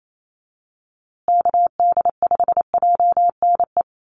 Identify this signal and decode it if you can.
Morse code, heard as a keyed tone, about 28 wpm, 700 Hz. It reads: KD5JNE
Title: USA Call Signs Repeated 3x 28wpm